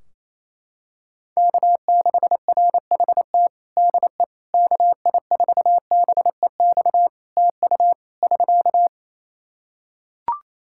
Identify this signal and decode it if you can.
Morse code, heard as a keyed tone, about 28 wpm, 700 Hz. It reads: K6RHT DE KI4BEX TU <SK>
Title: K1USN SST QSO Practice 28wpm